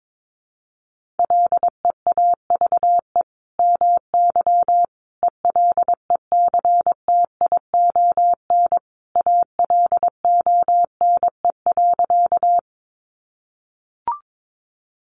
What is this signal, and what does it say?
Morse code, heard as a keyed tone, about 22 wpm, 700 Hz. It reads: LEAVE MY ELECTION ALONE.
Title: Sentences from Top 1500 Words 22wpm